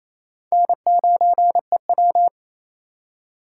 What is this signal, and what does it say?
Morse code, heard as a keyed tone, about 28 wpm, 700 Hz. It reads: N9EW
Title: USA Call Signs - 4 Characters - 2x Character Spacing - 28wpm